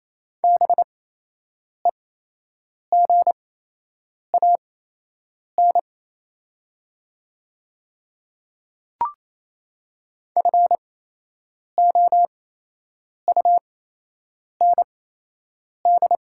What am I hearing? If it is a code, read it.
Morse code, heard as a keyed tone, about 28 wpm, spaced as for 8 wpm, 700 Hz. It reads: BEGAN FOUND
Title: 8x Character Spacing - Top 500 Words - 28wpm